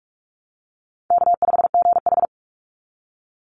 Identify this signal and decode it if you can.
Morse code, heard as a keyed tone, about 44 wpm, 700 Hz. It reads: K5GH
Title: Easy Call Signs Repeated 3x - 45wpm